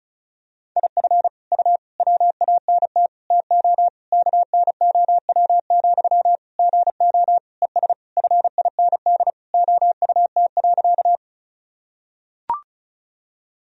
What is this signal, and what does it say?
Morse code, heard as a keyed tone, about 36 wpm, 700 Hz. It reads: IF U WANT TO KNOW, GO ES FIND OUT.
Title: Sentences from Top 200 Words and Abbreviations 35wpm